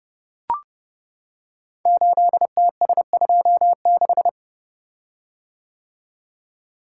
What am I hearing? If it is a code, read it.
Morse code, heard as a keyed tone, about 29 wpm, 700 Hz. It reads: 8TS26
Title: Code Groups - 5 Letter:Numbers - 30wpm